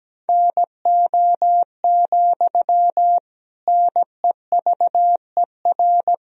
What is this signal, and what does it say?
Morse code, heard as a keyed tone, about 17 wpm, 700 Hz. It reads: NO, NEVER
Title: Sentences from Top 1000 Words - Review Part 1 - 17wpm